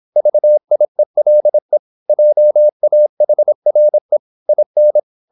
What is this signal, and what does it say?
Morse code, heard as a keyed tone, about 26 wpm, 600 Hz. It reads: VIELE JAHRE IN